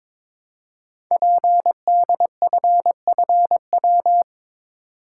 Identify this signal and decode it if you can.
Morse code, heard as a keyed tone, about 22 wpm, 700 Hz. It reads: PDFFW
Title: Code Groups - 5 Letter:Numbers - 22wpm